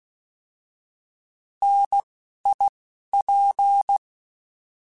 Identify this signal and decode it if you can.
Morse code, heard as a keyed tone, about 16 wpm, spaced as for 11 wpm, 800 Hz. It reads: NIP